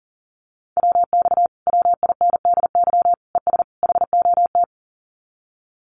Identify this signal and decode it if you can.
Morse code, heard as a keyed tone, about 40 wpm, 700 Hz. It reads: WX WINDY ES HOT
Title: Common QSO Phrases Encore 40wpm